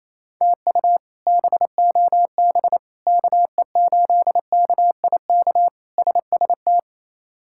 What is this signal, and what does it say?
Morse code, heard as a keyed tone, about 28 wpm, 700 Hz. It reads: TU BOB KE8KIK SST
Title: K1USN SST QSO Practice - 28wpm